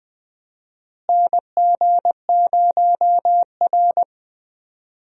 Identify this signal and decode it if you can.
Morse code, heard as a keyed tone, about 21 wpm, 700 Hz. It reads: NG0R
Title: USA Call Signs - 4 Characters - Repeated 2x - 20wpm